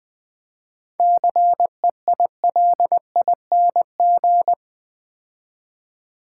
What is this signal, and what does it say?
Morse code, heard as a keyed tone, about 20 wpm, 700 Hz. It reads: CEILING
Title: 7 Letter Words - First 3 Letters Given - 20wpm